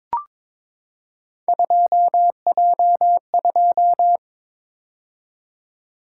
Morse code, heard as a keyed tone, about 22 wpm, 700 Hz. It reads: 2J2
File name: practice / 22wpm / Three Letter-Number Combinations - 22wpm